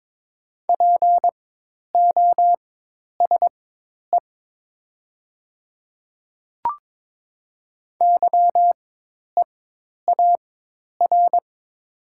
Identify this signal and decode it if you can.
Morse code, heard as a keyed tone, about 22 wpm, spaced as for 10 wpm, 700 Hz. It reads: POSE YEAR
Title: 4x Character Spacing - Top 500 Words - 22wpm